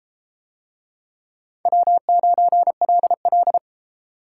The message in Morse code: W9LL

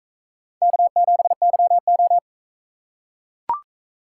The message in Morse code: K7YO

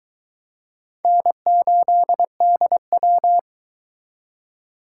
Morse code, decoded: N8DW

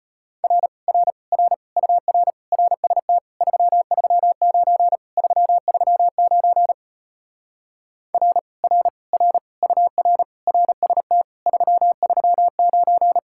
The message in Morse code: R R R UR RST 339 339 R R R UR RST 339

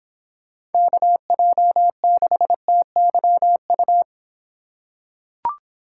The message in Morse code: KJ6TYU

